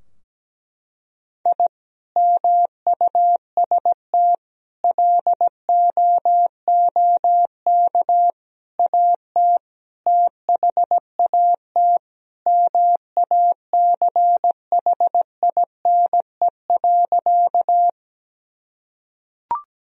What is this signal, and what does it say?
Morse code, heard as a keyed tone, about 17 wpm, 700 Hz. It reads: I MUST LOOK AT THAT MACHINE.
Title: Sentences from Top 1000 Words - Review Part 2 - 17wpm